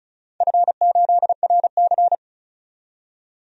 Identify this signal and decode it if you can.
Morse code, heard as a keyed tone, about 34 wpm, 700 Hz. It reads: F8RC
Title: DXCC-100 Call Signs - 4 Characters - 35wpm